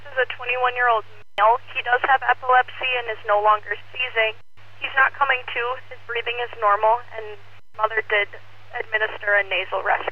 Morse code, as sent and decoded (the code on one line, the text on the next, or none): none